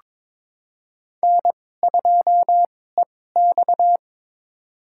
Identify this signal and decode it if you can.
Morse code, heard as a keyed tone, about 23 wpm, spaced as for 16 wpm, 700 Hz. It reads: N2EX
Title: USA Call Signs - 4 Characters - 2x Character Spacing - 22wpm